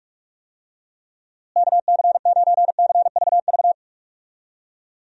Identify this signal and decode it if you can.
Morse code, heard as a keyed tone, about 43 wpm, 700 Hz. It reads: KC9CVV
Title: USA Call Signs - 6 Characters - Repeated 3x - 45wpm